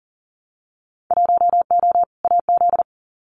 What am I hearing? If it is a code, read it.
Morse code, heard as a keyed tone, about 40 wpm, 700 Hz. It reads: 1O AZ
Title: Winter Field Day Exchanges 40wpm